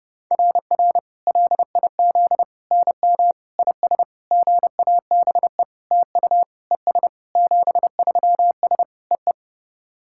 Morse code, heard as a keyed tone, about 31 wpm, 690 Hz. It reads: RR LIZ NM IS GABE TU ES 73S EE
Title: SOTA QSO Practice 30wpm